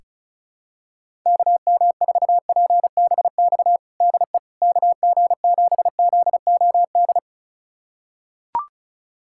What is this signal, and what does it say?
Morse code, heard as a keyed tone, about 36 wpm, 700 Hz. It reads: KM4PBX DE KG7ZOD
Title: USA Callsign DE Callsign - 6 Characters - 35wpm